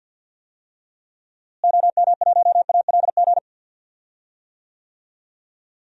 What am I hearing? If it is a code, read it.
Morse code, heard as a keyed tone, about 50 wpm, 700 Hz. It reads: OM1ALG
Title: DXCC-100 Call Signs - 6 Characters - Repeated 2x - 50wpm